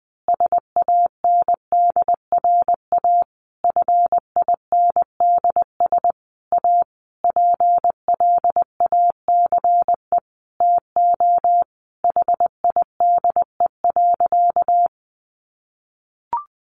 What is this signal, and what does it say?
Morse code, heard as a keyed tone, about 20 wpm, 700 Hz. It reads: SANDRA FINDS A PLACE TO HIDE.
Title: Easy Phrases 20wpm